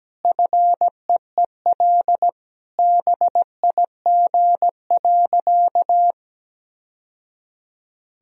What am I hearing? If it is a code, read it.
Morse code, heard as a keyed tone, about 17 wpm, 700 Hz. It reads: FEEL BIG.
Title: Sentences from Top 1000 Words - Review Part 1 - 17wpm